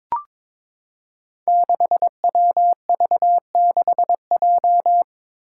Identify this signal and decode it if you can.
Morse code, heard as a keyed tone, about 23 wpm, 700 Hz. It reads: 6WV6J
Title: Five Challenging Letter-Number Combinations 22wpm